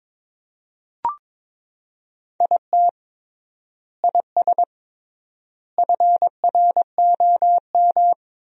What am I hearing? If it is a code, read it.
Morse code, heard as a keyed tone, about 22 wpm, 700 Hz. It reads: IT IS FROM
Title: 3x Word-Spacing - Sentences from Top 300 Words - 22wpm